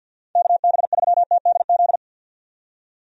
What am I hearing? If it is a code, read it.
Morse code, heard as a keyed tone, about 52 wpm, 700 Hz. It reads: KB3TDZ
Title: USA Call Signs - 6 Characters - 50wpm